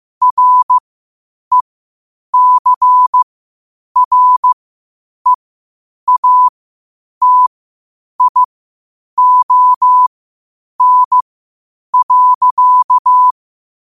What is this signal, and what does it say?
Morse code, heard as a keyed tone, about 15 wpm, spaced as for 8 wpm, 1000 Hz. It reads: RECREATION.
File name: qotdfiles / 8wpm